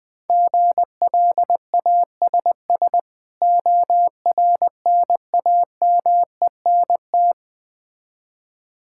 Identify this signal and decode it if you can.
Morse code, heard as a keyed tone, about 20 wpm, 700 Hz. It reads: GLASS ORNAMENT
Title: Difficult Words with Hints - 20wpm